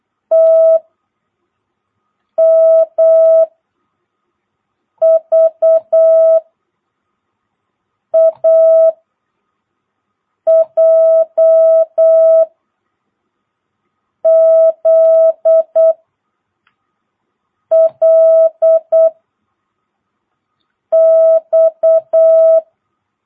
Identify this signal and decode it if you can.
Morse code, heard as a keyed tone, about 8 wpm, 650 Hz. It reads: T M V A J Z L X
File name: jh8cbh / 2022cw